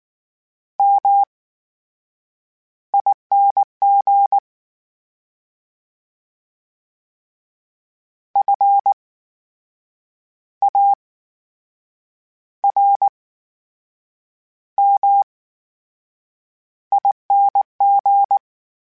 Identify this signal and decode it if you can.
Morse code, heard as a keyed tone, about 19 wpm, spaced as for 7 wpm, 800 Hz. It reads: MING FARMING
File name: Special INGs / Special INGs-mp3s / VerySlowToFaster Ing @18